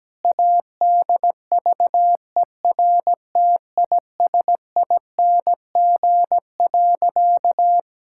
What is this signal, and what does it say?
Morse code, heard as a keyed tone, about 17 wpm, 700 Hz. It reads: ADVERTISING.
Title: Sentences from Top 1000 Words - Review Part 6 - 17wpm